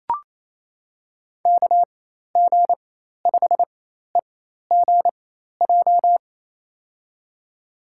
Morse code, heard as a keyed tone, about 28 wpm, spaced as for 13 wpm, 700 Hz. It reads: KG5EGJ
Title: USA Call Signs - 4x Character Spacing - 28wpm